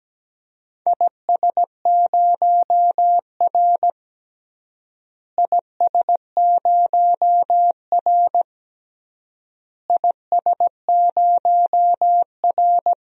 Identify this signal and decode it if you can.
Morse code, heard as a keyed tone, about 17 wpm, 700 Hz. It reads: IS0R IS0R IS0R
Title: DXCC-100 Call Signs - 4 Characters - Repeated 3x - 17wpm